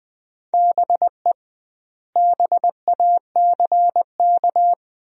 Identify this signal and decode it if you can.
Morse code, heard as a keyed tone, about 20 wpm, 700 Hz. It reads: BE BACK
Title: 2x Word-Spacing - Sentences from Top 500 Words - 20wpm